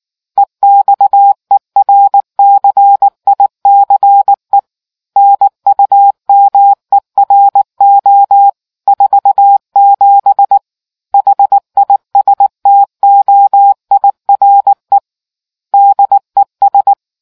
Morse code, heard as a keyed tone, about 19 wpm, 800 Hz. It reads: EXERCICE NUMERO 47 HISTOIRE DES